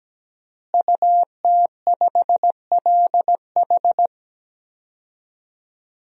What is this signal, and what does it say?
Morse code, heard as a keyed tone, about 17 wpm, 700 Hz. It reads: UT5LH
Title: DXCC-100 Call Signs - 5 Characters - Repeated 3x - 17wpm